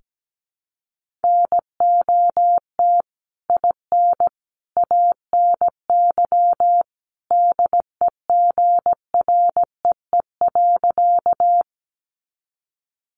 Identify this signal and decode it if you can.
Morse code, heard as a keyed tone, about 17 wpm, 700 Hz. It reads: NOT IN ANY DEGREE.
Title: Sentences from Top 1500 Words 17wpm